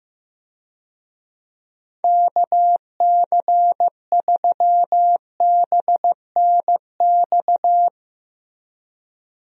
Morse code, heard as a keyed tone, about 15 wpm, 700 Hz. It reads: KC3BNX
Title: USA Call Signs - 6 Characters - Repeated 2x - 15wpm